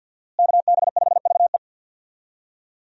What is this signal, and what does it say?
Morse code, heard as a keyed tone, about 49 wpm, 690 Hz. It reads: KB5VE
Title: USA Call Signs - 5 Characters - Repeated 3x - 50wpm